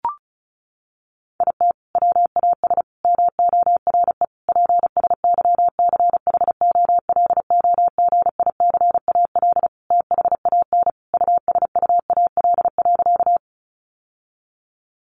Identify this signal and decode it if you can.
Morse code, heard as a keyed tone, about 34 wpm, 700 Hz. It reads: IT WAS MORE PSYCHOLOGICAL THAN USUAL.
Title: Sentences from Top 1400 Words 35wpm